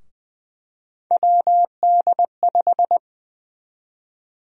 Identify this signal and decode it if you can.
Morse code, heard as a keyed tone, about 20 wpm, 700 Hz. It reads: WD5
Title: USA Call Sign Prefixes - 20wpm